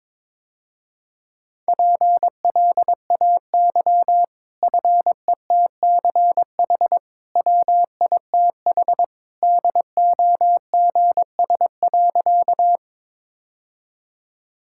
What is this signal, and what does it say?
Morse code, heard as a keyed tone, about 22 wpm, 700 Hz. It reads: PLAY FETCH WITH DOGS.
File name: practice / 22wpm / Easy Sentences with Hints - 22wpm